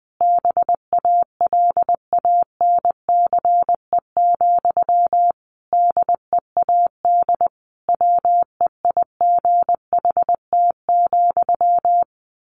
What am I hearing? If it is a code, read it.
Morse code, heard as a keyed tone, about 20 wpm, 700 Hz. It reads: BALANCE, DEAD WEIGHT,
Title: A Sound Of Thunder 20wpm